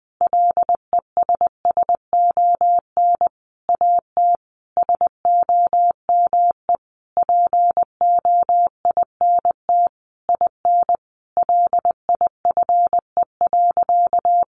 Morse code, heard as a keyed tone, about 20 wpm, 700 Hz. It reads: LESSON AT SOME POINT IN LIFE.
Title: Sentences from Top 1200 Words 20wpm